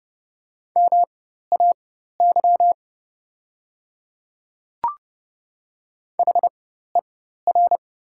Morse code, heard as a keyed tone, about 30 wpm, spaced as for 14 wpm, 700 Hz. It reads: MAY HER
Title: 4x Character Spacing - Top 200 Words - 30wpm